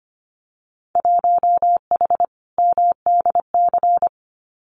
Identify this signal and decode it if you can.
Morse code, heard as a keyed tone, about 25 wpm, 700 Hz. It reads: 1H MDC
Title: Winter Field Day Exchanges 25wpm